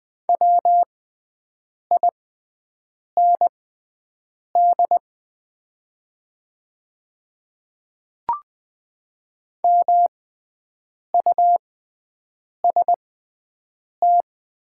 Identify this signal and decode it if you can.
Morse code, heard as a keyed tone, about 20 wpm, spaced as for 7 wpm, 700 Hz. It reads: WIND MUST
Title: 6x Character Spacing - Top 500 Words - 20wpm